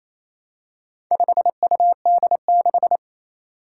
Morse code, heard as a keyed tone, about 27 wpm, 700 Hz. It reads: 5UD6
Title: Four Challenging Letter-Number Combinations 28wpm